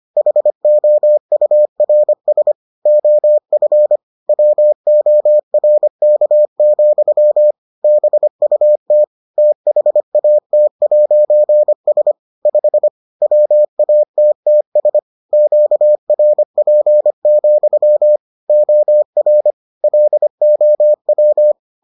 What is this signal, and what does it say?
Morse code, heard as a keyed tone, about 25 wpm, 600 Hz. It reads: HOURS OF WORK, BUT THAT'S 5 WATTS QRP, OR LOW